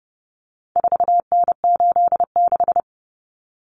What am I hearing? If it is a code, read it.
Morse code, heard as a keyed tone, about 28 wpm, 700 Hz. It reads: 4N86
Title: Common Cut Numbers - 4-Digit 30wpm